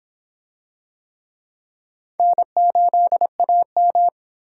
Morse code, heard as a keyed tone, about 25 wpm, 700 Hz. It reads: N8AM